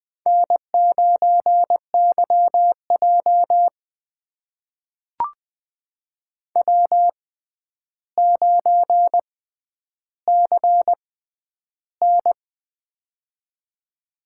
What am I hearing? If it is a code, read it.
Morse code, heard as a keyed tone, about 20 wpm, spaced as for 10 wpm, 700 Hz. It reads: N9YJ W9CN
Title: USA Call Signs - 6x Character Spacing - 20wpm